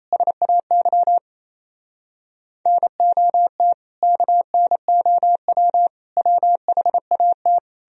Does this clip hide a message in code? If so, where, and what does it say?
Yes: Morse code from 0.1 to 7.6 s, heard as a keyed tone, about 37 wpm, 700 Hz. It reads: SAY NOT KNOW WHAT